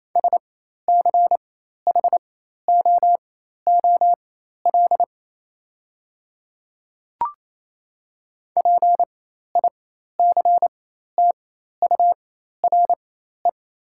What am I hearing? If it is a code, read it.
Morse code, heard as a keyed tone, about 28 wpm, spaced as for 13 wpm, 700 Hz. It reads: SCHOOL PICTURE
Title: 4x Character Spacing - Top 500 Words - 28wpm